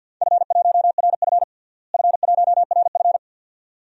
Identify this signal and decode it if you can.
Morse code, heard as a keyed tone, about 51 wpm, 700 Hz. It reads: F1RF F1RF
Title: DXCC-100 Call Signs - 4 Characters - Repeated 2x - 50wpm